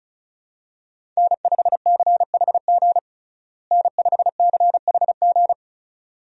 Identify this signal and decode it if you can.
Morse code, heard as a keyed tone, about 36 wpm, 700 Hz. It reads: N5CHG N5CHG
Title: USA Call Signs - 5 Characters - Repeated 2x - 35wpm